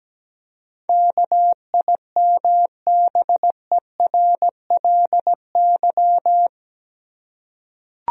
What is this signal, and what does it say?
Morse code, heard as a keyed tone, about 17 wpm, 700 Hz. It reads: KIMBERLY
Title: USA Common Names - 17wpm